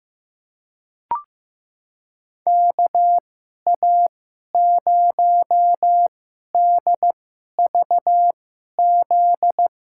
Morse code, heard as a keyed tone, about 15 wpm, spaced as for 11 wpm, 700 Hz. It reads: KA0DVZ
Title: USA Call Signs - 6 Characters - 2x Character Spacing - 15wpm